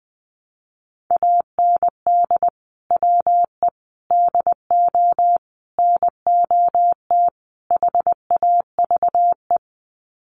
Morse code, heard as a keyed tone, about 20 wpm, 700 Hz. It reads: AND WE DO NOT HAVE
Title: Sets of 5 Words Encore 20wpm